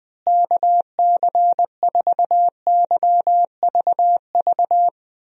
Morse code, heard as a keyed tone, about 20 wpm, 700 Hz. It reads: KC4YVV
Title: USA Call Signs - 6 Characters - Repeated 3x - 20wpm